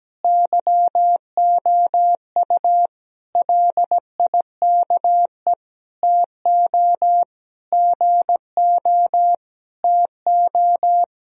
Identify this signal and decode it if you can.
Morse code, heard as a keyed tone, about 17 wpm, 700 Hz. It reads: YOU LIKE TO GO TO
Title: Sentences from Top 1000 Words - Review Part 1 - 17wpm